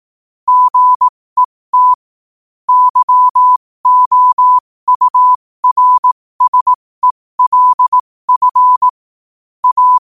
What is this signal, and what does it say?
Morse code, heard as a keyed tone, about 18 wpm, spaced as for 15 wpm, 1000 Hz. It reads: GET YOURSELF A
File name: qotdfiles / 15wpm